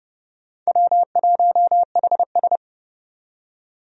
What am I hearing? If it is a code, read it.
Morse code, heard as a keyed tone, about 29 wpm, 700 Hz. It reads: W1HS
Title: USA Call Signs - 4 Characters - 30wpm